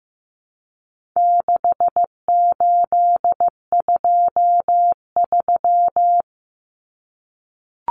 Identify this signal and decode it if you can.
Morse code, heard as a keyed tone, about 15 wpm, 700 Hz. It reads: 6823
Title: Numbers - 4-Digit 15wpm